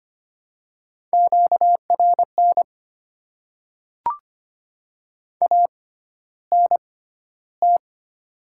Morse code, heard as a keyed tone, about 25 wpm, spaced as for 13 wpm, 700 Hz. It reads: QRN ANT
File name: Top 90 QSO Elements - 6x Character Spacing - 25wpm